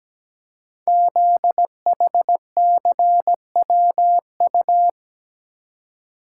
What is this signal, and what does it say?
Morse code, heard as a keyed tone, about 17 wpm, 700 Hz. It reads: ZHCWU
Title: Code Groups - 5 Letters - 17wpm